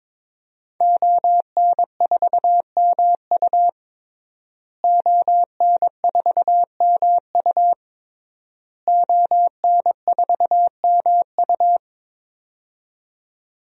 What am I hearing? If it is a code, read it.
Morse code, heard as a keyed tone, about 21 wpm, 700 Hz. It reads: ON4MU ON4MU ON4MU
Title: DXCC-100 Call Signs - 5 Characters - Repeated 3x - 22wpm